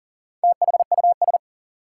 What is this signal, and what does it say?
Morse code, heard as a keyed tone, about 40 wpm, 700 Hz. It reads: THUS
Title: Sentences from Top 800 Words - New Word Drill - 40wpm